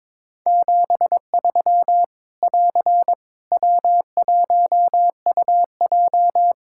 Morse code, heard as a keyed tone, about 21 wpm, 700 Hz. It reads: 73 <AR> W1UJ
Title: Practice QSO - Exchange 4 - Ending - 22wpm